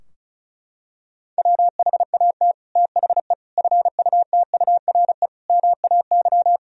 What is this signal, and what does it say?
Morse code, heard as a keyed tone, about 36 wpm, 700 Hz. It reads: WHAT THE FUTURE MAY
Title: Sentences from Top 1000 Words - Review Part 2 - 35wpm